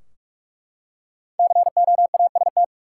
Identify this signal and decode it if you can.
Morse code, heard as a keyed tone, about 43 wpm, 700 Hz. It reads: COAST
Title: Five Letter Words - 45wpm